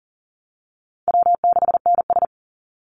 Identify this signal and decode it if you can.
Morse code, heard as a keyed tone, about 40 wpm, 700 Hz. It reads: W6NS
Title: Easy Call Signs - 40wpm